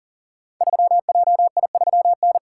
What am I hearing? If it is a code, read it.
Morse code, heard as a keyed tone, about 38 wpm, 700 Hz. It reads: 3JI3N
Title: Code Groups - 5 Letter:Numbers - 40wpm